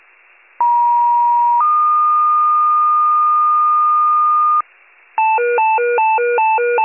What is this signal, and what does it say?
Two-tone page, heard as a one-second tone, then a three-second tone, about 950 then 1220 Hz, with noise. It is heard from 0.6 to 4.6 s.